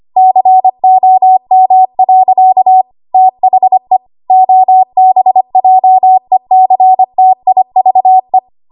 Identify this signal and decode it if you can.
Morse code, heard as a keyed tone, about 25 wpm, 750 Hz. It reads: COM. THE OBJECTIVE
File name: Archive / 25 WPM / 25WPM